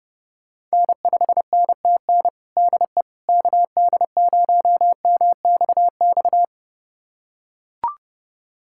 Morse code, heard as a keyed tone, about 30 wpm, 700 Hz. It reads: N5NTN DE KD0MXX
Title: USA Callsign DE Callsign 30wpm